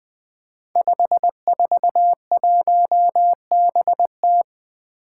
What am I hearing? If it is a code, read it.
Morse code, heard as a keyed tone, about 20 wpm, 700 Hz. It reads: 541BT
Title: Five Challenging Letter-Number Combinations 20wpm